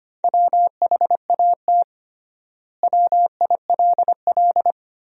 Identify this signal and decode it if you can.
Morse code, heard as a keyed tone, about 25 wpm, 700 Hz. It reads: WHAT WILL
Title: 3x Word-Spacing - Sentences from Top 400 Words - 25wpm